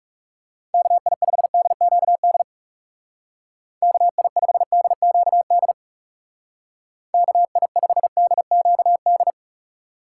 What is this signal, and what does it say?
Morse code, heard as a keyed tone, about 37 wpm, 700 Hz. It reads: KI5DQD KI5DQD KI5DQD
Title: Speed Racing - USA Call Signs 30wpm